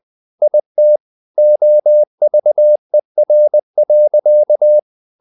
Morse code, heard as a keyed tone, about 20 wpm, 600 Hz. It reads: IT OVER.
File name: the old man and the sea part 2 20wpm